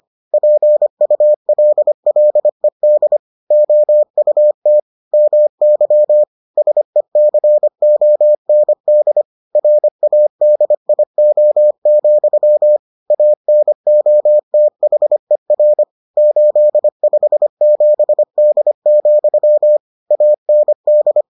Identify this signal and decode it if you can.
Morse code, heard as a keyed tone, about 25 wpm, 600 Hz. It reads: PULLED OUT MY SECOND RADIO, ANOTHER 857D, AND